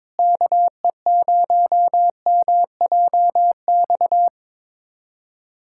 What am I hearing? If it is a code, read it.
Morse code, heard as a keyed tone, about 22 wpm, 700 Hz. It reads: KE0MJX
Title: USA Callsign DE Callsign - 6 Characters - 22wpm